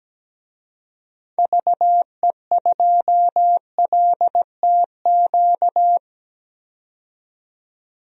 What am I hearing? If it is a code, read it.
Morse code, heard as a keyed tone, about 17 wpm, 700 Hz. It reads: VE2LTQ
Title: DXCC-100 Call Signs - 6 Characters - 17wpm